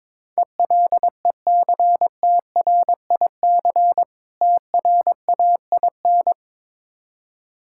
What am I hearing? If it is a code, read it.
Morse code, heard as a keyed tone, about 22 wpm, 690 Hz. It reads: ELECTRIC TRAIN
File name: Difficult Words with Hints - 22wpm